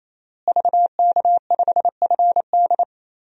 Morse code, heard as a keyed tone, about 28 wpm, 700 Hz. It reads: VK5FD
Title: DXCC-100 Call Signs - 5 Characters - Repeated 3x - 28wpm